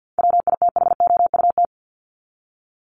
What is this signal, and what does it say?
Morse code, heard as a keyed tone, about 50 wpm, 700 Hz. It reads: WITHOUT
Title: Top 501-750 Words 50wpm